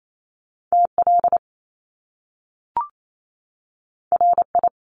Morse code, heard as a keyed tone, about 28 wpm, 700 Hz. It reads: TL RI
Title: Top Two Letter Combinations in Words 28wpm